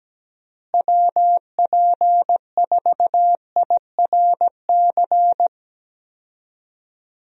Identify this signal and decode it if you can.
Morse code, heard as a keyed tone, about 17 wpm, 700 Hz. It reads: WP4IRC
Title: USA Call Signs - 6 Characters - Repeated 3x - 17wpm